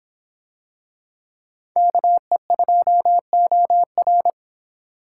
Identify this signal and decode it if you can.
Morse code, heard as a keyed tone, about 23 wpm, 700 Hz. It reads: KE2OR